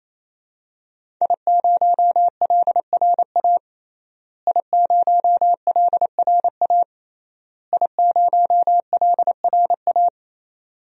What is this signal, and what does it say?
Morse code, heard as a keyed tone, about 28 wpm, 690 Hz. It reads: I0LRA I0LRA I0LRA
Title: DXCC-100 Call Signs - 5 Characters - Repeated 3x - 28wpm